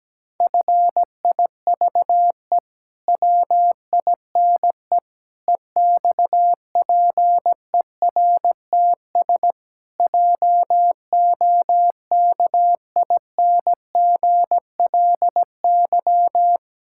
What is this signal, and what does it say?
Morse code, heard as a keyed tone, about 17 wpm, 700 Hz. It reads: FIVE WINE EXPERTS JOKINGLY
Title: Pangram Challenge v2 - 17wpm